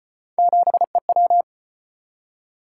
Morse code, heard as a keyed tone, about 38 wpm, 700 Hz. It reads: 7EW